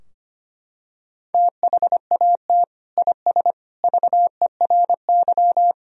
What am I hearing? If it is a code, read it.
Morse code, heard as a keyed tone, about 25 wpm, 700 Hz. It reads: THAT IS VERY
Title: Sentences from Top 1000 Words - Review Part 2 - 25wpm